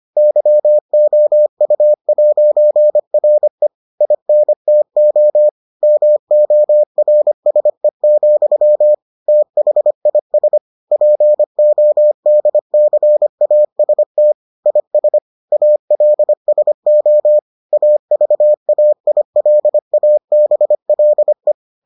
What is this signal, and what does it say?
Morse code, heard as a keyed tone, about 25 wpm, 600 Hz. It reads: YOU'RE INTO MORSE, THIS PODCAST IS ALSO AVAILABLE